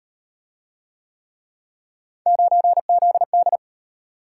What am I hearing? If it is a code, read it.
Morse code, heard as a keyed tone, about 38 wpm, 700 Hz. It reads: 9ZD